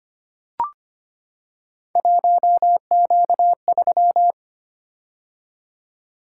Morse code, heard as a keyed tone, about 25 wpm, 700 Hz. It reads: 1Q3
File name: Three Letter-Number Combinations - 25wpm